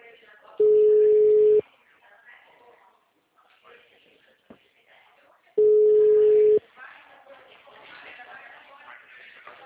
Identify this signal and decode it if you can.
Ringback tone (Europe, 425 Hz), with speech.
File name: out- -2009- - - .400